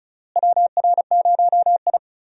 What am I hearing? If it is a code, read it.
Morse code, heard as a keyed tone, about 34 wpm, 700 Hz. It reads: WR0I
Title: USA Call Signs - 4 Characters - Repeated 3x - 35wpm